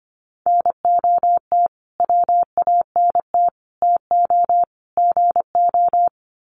Morse code, heard as a keyed tone, about 25 wpm, 700 Hz. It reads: NOT WANT TO GO
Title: Sets of 4 Words 25wpm